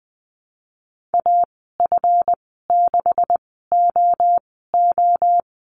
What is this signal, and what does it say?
Morse code, heard as a keyed tone, about 20 wpm, spaced as for 14 wpm, 700 Hz. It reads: AF6OO
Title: 147 Lesson 24g - Letter y - Call Signs 20wpm